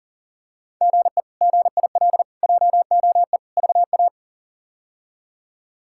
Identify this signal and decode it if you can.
Morse code, heard as a keyed tone, about 38 wpm, 700 Hz. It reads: GE GIL JOE VA
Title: K1USN SST QSO Practice - 40wpm